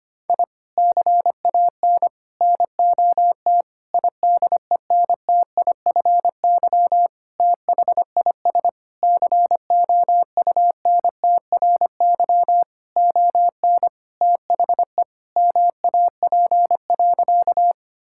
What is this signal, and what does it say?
Morse code, heard as a keyed tone, about 25 wpm, 700 Hz. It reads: I CAN NOT IDENTIFY THIS COUNTRY ON THE MAP.
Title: Sentences from Top 1000 Words - Review Part 6 - 25wpm